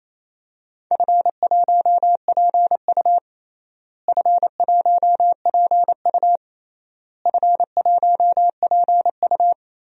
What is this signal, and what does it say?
Morse code, heard as a keyed tone, about 28 wpm, 700 Hz. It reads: F1PU F1PU F1PU
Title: DXCC-100 Call Signs - 4 Characters - Repeated 3x - 28wpm